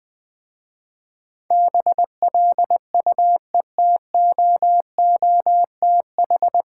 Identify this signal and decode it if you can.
Morse code, heard as a keyed tone, about 20 wpm, 700 Hz. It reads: BLUETOOTH
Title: Difficult Words with Hints - 20wpm